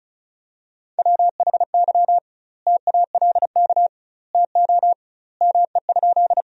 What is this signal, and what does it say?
Morse code, heard as a keyed tone, about 36 wpm, 700 Hz. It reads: WHY TALK TO ME?
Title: 2x Word-Spacing - Sentences from Top 400 Words - 35wpm